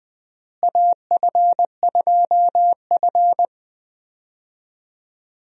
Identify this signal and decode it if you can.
Morse code, heard as a keyed tone, about 20 wpm, 700 Hz. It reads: AF2F
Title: USA Call Signs - 4 Characters - Repeated 3x - 20wpm